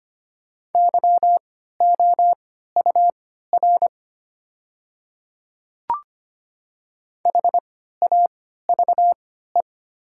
Morse code, heard as a keyed tone, about 25 wpm, spaced as for 14 wpm, 700 Hz. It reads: YOUR HAVE
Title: 3x Character Spacing - Top 100 Words - 25wpm